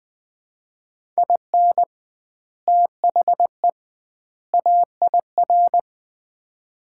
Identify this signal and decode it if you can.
Morse code, heard as a keyed tone, about 20 wpm, 700 Hz. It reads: IN THE AIR
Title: 2x Word-Spacing - Sets of 3 Words - 20wpm